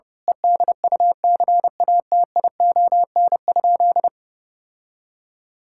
Morse code, heard as a keyed tone, about 30 wpm, 700 Hz. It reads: EDUCATION?
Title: Sentences from Top 1000 Words - Review Part 6 - 30wpm